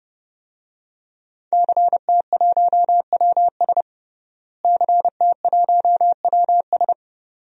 Morse code, heard as a keyed tone, about 32 wpm, 700 Hz. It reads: CT1WS CT1WS
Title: DXCC-100 Call Signs - 5 Characters - Repeated 2x - 30wpm